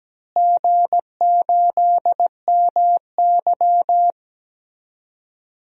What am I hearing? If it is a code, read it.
Morse code, heard as a keyed tone, about 17 wpm, 700 Hz. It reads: G8MY
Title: DXCC-100 Call Signs - 4 Characters - Repeated 3x - 17wpm